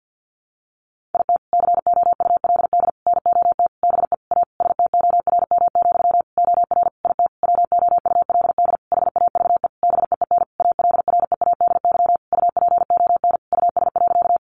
Morse code, heard as a keyed tone, about 50 wpm, 700 Hz. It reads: IT COULD NOT BE A STORM, OR IT WOULD HAVE BEEN ALREADY UPON US.
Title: Sentences from Top 1500 Words 50wpm